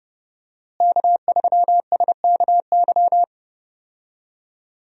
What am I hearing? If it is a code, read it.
Morse code, heard as a keyed tone, about 30 wpm, 700 Hz. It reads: K3SKY
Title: USA Call Signs - 5 Characters - Repeated 2x - 30wpm